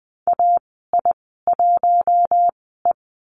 Morse code, heard as a keyed tone, about 20 wpm, spaced as for 14 wpm, 700 Hz. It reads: AI1E
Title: 082 Lesson 14g - Number 2 - Call Signs 20wpm